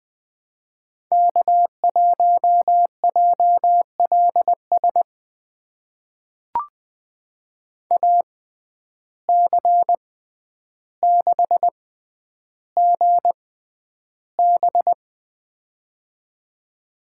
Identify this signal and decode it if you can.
Morse code, heard as a keyed tone, about 20 wpm, spaced as for 10 wpm, 700 Hz. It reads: K1JLS AC6GB
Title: USA Call Signs - 5 Characters - 6x Character Spacing - 20wpm